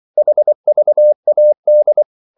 Morse code, heard as a keyed tone, about 24 wpm, 600 Hz. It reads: HVAD
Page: Ferie 24 wpm